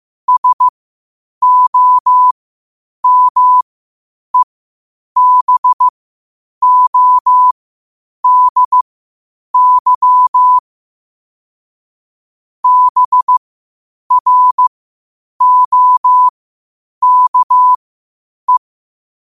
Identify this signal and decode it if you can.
Morse code, heard as a keyed tone, about 15 wpm, spaced as for 8 wpm, 1000 Hz. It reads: SOMEBODY BROKE